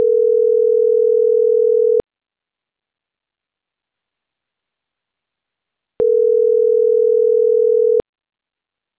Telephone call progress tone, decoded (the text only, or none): ringback tone